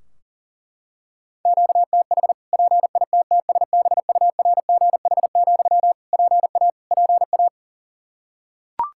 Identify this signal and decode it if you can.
Morse code, heard as a keyed tone, about 40 wpm, 700 Hz. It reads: QTH PITTSBURGH, PA PA
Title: Practice QSO - Exchange 1 - Beginning - 40wpm